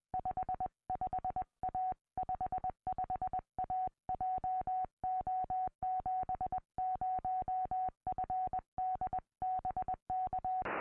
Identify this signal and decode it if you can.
Morse code, heard as a keyed tone, about 20 wpm, 740 Hz. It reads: 55A 55A JO70FD BK
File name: au / sm7gvf 2m